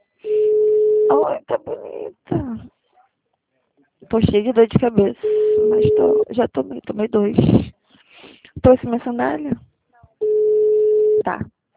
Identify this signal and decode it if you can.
Ringback tone (Europe, 425 Hz), with speech.
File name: out- -2005- - - .41582